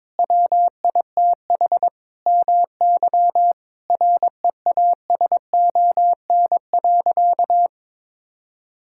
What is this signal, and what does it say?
Morse code, heard as a keyed tone, about 22 wpm, 700 Hz. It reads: WITH MY REASON.
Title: Sentences from Top 1000 Words - Review Part 5 - 22wpm